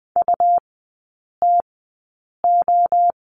Morse code, heard as a keyed tone, about 20 wpm, spaced as for 8 wpm, 700 Hz. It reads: UTO
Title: 133 Lesson 22e - Question Mark - Three Characters 20wpm